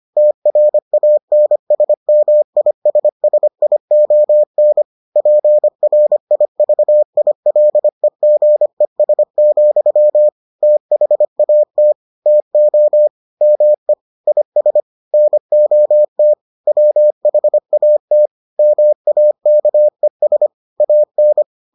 Morse code, heard as a keyed tone, about 25 wpm, 600 Hz. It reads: TRANSMISSION PRIVILEGES, THAT TO ME IS NOT WHAT MAKES AN